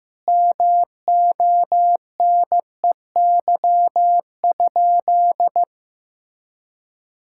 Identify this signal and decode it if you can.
Morse code, heard as a keyed tone, about 16 wpm, 700 Hz. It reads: MONEY?
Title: Sentences from Top 1000 Words - Review Part 6 - 15wpm